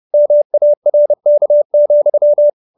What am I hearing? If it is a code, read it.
Morse code, heard as a keyed tone, about 30 wpm, 570 Hz. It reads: MARK,